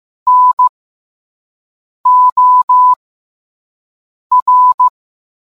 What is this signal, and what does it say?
Morse code, heard as a keyed tone, about 15 wpm, spaced as for 5 wpm, 1000 Hz. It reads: NOR